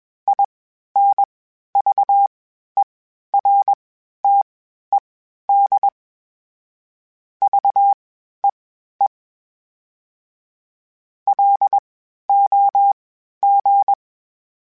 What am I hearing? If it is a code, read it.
Morse code, heard as a keyed tone, about 21 wpm, spaced as for 12 wpm, 800 Hz. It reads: INVERTED VEE LOG
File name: (spaced) / FromSMs spaced mp3s / Antennas sp@25